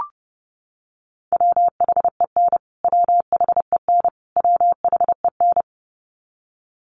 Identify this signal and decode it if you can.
Morse code, heard as a keyed tone, about 30 wpm, 700 Hz. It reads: WHEN WHEN WHEN
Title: Top 200 Words Repeated 30wpm